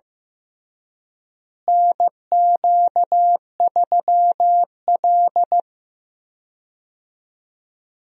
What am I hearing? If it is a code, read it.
Morse code, heard as a keyed tone, about 15 wpm, 700 Hz. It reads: NQ3L
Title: USA Call Signs - 4 Characters - Repeated 2x - 15wpm